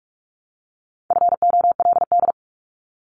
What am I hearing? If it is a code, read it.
Morse code, heard as a keyed tone, about 46 wpm, 700 Hz. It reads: FOLD
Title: Multi-letter Phonemes 45wpm